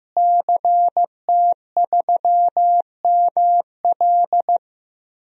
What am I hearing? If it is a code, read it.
Morse code, heard as a keyed tone, about 15 wpm, 700 Hz. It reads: CT3ML
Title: DXCC-100 Call Signs - 5 Characters - Repeated 3x - 15wpm